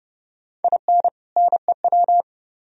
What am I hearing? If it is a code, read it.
Morse code, heard as a keyed tone, about 37 wpm, 700 Hz. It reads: IN NEW
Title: Speed-Racing - Sets of 2 Words 30wpm